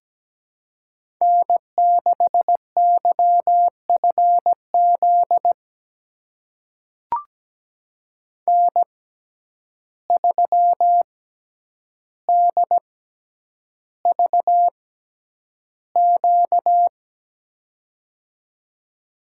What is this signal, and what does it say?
Morse code, heard as a keyed tone, about 17 wpm, spaced as for 9 wpm, 700 Hz. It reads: N6YFZ N3DVQ
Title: USA Call Signs - 5 Characters - 6x Character Spacing - 17wpm